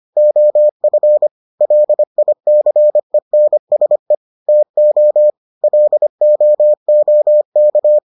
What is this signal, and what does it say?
Morse code, heard as a keyed tone, about 25 wpm, 600 Hz. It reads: OF LICENSE TO LOOK